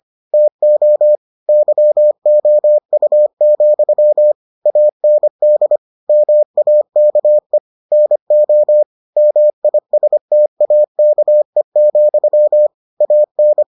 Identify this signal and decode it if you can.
Morse code, heard as a keyed tone, about 25 wpm, 600 Hz. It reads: TO YOU, AND MAKE NO MISTAKE, AN